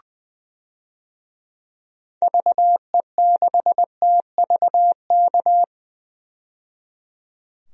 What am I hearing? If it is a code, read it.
Morse code, heard as a keyed tone, about 20 wpm, 700 Hz. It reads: VE6TVK
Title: DXCC-100 Call Signs - 6 Characters - 20wpm